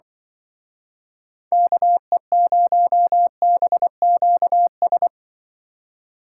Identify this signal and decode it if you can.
Morse code, heard as a keyed tone, about 24 wpm, 700 Hz. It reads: KE0BQS
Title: Speed Racing - USA 2x3 Call Signs 20wpm